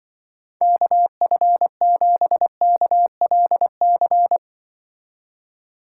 Morse code, heard as a keyed tone, about 24 wpm, 700 Hz. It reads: KF7KLC
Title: Speed Racing - USA 2x3 Call Signs 20wpm